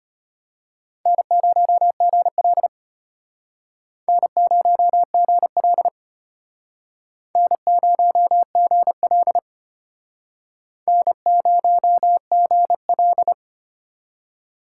Morse code, heard as a keyed tone, about 32 wpm, 700 Hz. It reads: N0GL N0GL N0GL N0GL